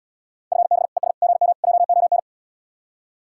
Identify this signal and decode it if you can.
Morse code, heard as a keyed tone, about 77 wpm, 700 Hz. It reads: FB ES GM JON